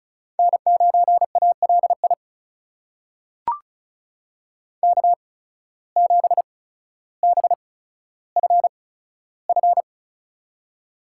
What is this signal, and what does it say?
Morse code, heard as a keyed tone, about 36 wpm, spaced as for 15 wpm, 700 Hz. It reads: N9ALI K7BFF
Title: USA Call Signs - 5 Characters - 8x Character Spacing - 35wpm